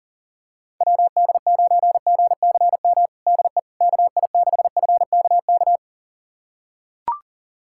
Morse code, heard as a keyed tone, about 40 wpm, 700 Hz. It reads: WD9GCM DE KI6FKK
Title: USA Callsign DE Callsign - 6 Characters - 40wpm